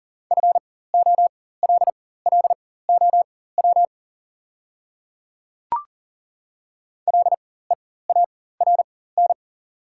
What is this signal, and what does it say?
Morse code, heard as a keyed tone, about 40 wpm, spaced as for 19 wpm, 690 Hz. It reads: FOLLOW LEARN
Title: 4x Character Spacing - Top 300 Words - 40wpm